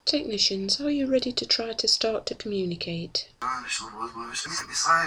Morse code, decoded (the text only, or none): none